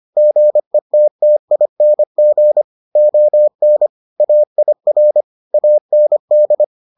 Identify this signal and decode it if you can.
Morse code, heard as a keyed tone, about 25 wpm, 600 Hz. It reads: GETTING ON AIR AND